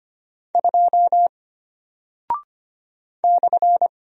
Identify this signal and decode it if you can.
Morse code, heard as a keyed tone, about 25 wpm, 700 Hz. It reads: 2 /
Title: Single Character (Letter:Number:Punctuation:Prosign) Alt 25wpm